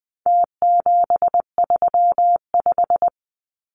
Morse code, heard as a keyed tone, about 20 wpm, 700 Hz. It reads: T735
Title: Numbers - Contest Style - 4-Digit 20wpm